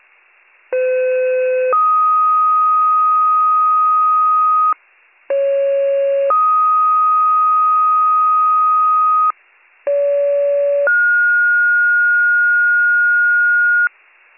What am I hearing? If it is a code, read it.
Two-tone page, heard as a one-second tone, then a three-second tone, about 510 then 1250 Hz, about 550 then 1250 Hz, about 570 then 1430 Hz. It is heard from 0.7 to 13.9 s.